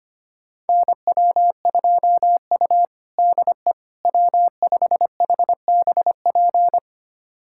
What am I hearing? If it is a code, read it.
Morse code, heard as a keyed tone, about 25 wpm, 700 Hz. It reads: NW2U DE W5HBP
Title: USA Callsign DE Callsign 25wpm